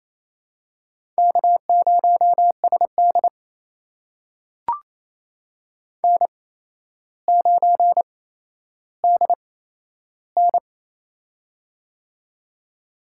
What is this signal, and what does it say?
Morse code, heard as a keyed tone, about 28 wpm, spaced as for 12 wpm, 700 Hz. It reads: K0SD N9DN
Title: USA Call Signs - 4 Characters - 8x Character Spacing - 28wpm